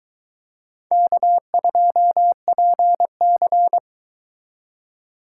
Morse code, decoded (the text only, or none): K2PC